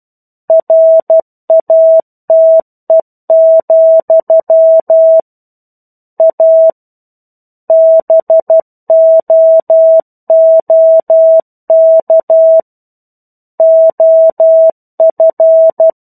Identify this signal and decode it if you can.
Morse code, heard as a keyed tone, about 12 wpm, 650 Hz. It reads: RATE, A BOOK OF
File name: 01-Chapter-01 12 WPM